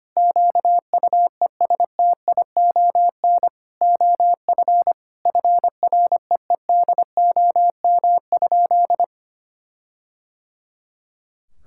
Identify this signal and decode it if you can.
Morse code, heard as a keyed tone, about 25 wpm, 700 Hz. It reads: QUESTION OF FREEDOM?
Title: Sentences from Top 1600 Words 25wpm